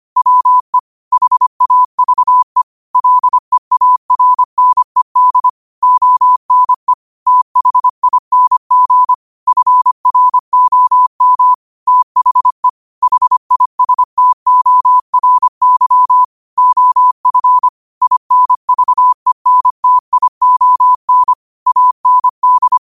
Morse code, heard as a keyed tone, about 25 wpm, 1000 Hz. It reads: WE HAVE LEARNED ONE THING FROM THE HISTORY OF INVENTION AND
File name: qotdfiles / 25wpm